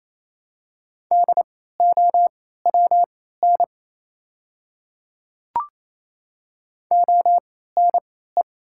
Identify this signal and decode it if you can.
Morse code, heard as a keyed tone, about 28 wpm, spaced as for 16 wpm, 700 Hz. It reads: DOWN ONE
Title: 3x Character Spacing - Top 100 Words - 28wpm